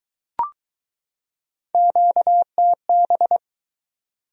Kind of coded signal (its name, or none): Morse code